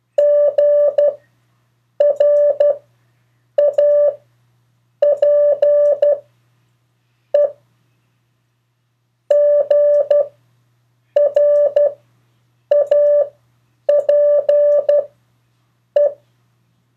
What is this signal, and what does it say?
Morse code, heard as a keyed tone, about 12 wpm, 580 Hz. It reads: G R A P E G R A P E
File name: 3of4-words-2